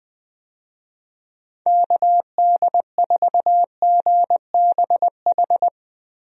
Morse code, heard as a keyed tone, about 20 wpm, 700 Hz. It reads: KD4GBH
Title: USA Call Signs - 6 Characters - Repeated 2x - 20wpm